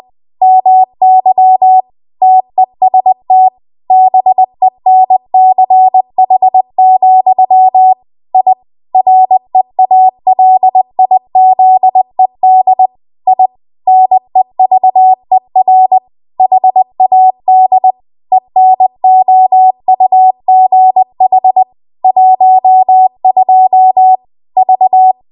Morse code, heard as a keyed tone, about 20 wpm, 750 Hz. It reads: MY TEST BENCH, I REALIZED I NEVER HAD ENOUGH 12 V
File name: Archive / 20 WPM / 20WPM